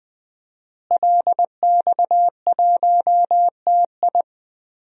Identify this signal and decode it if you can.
Morse code, heard as a keyed tone, about 20 wpm, 700 Hz. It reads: LX1TI
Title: DXCC-100 Call Signs - 5 Characters - Repeated 3x - 20wpm